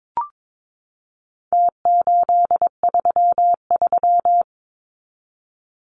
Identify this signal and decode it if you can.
Morse code, heard as a keyed tone, about 22 wpm, 700 Hz. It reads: T833
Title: Numbers - Contest Style - 4-Digit 22wpm